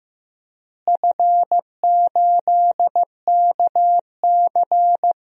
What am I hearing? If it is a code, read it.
Morse code, heard as a keyed tone, about 15 wpm, 700 Hz. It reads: F8KC
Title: DXCC-100 Call Signs - 4 Characters - Repeated 3x - 15wpm